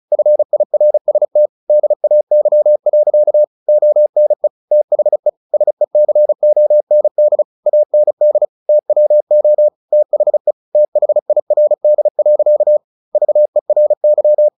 Morse code, heard as a keyed tone, about 34 wpm, 600 Hz. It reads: FIRST DAY. ONE THE SECOND AND TWO THE THIRD. VERY
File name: the old man and the sea 2 35wpm